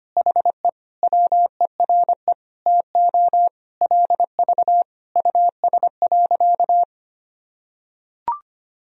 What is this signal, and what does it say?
Morse code, heard as a keyed tone, about 25 wpm, 700 Hz. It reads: HE WERE TO LV US.
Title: Sentences from Top 400 Words and Abbreviations 25wpm